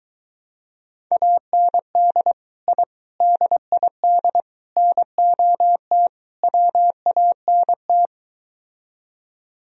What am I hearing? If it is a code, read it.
Morse code, heard as a keyed tone, about 23 wpm, 700 Hz. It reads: AND I DID NOT WANT